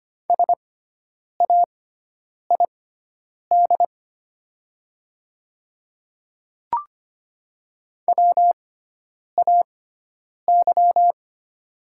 Morse code, heard as a keyed tone, about 25 wpm, spaced as for 9 wpm, 700 Hz. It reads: SAID WAY
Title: 6x Character Spacing - Top 100 Words - 25wpm